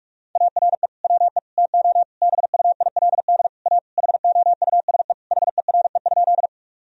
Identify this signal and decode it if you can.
Morse code, heard as a keyed tone, about 41 wpm, 700 Hz. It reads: ARE WE TO BUILD A HOUSE HERE?
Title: Sentences from Top 1000 Words - Review Part 1 - 45wpm